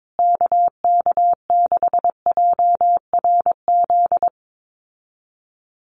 Morse code, heard as a keyed tone, about 22 wpm, 700 Hz. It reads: KK6JRZ
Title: USA Call Signs Repeated 2x 22wpm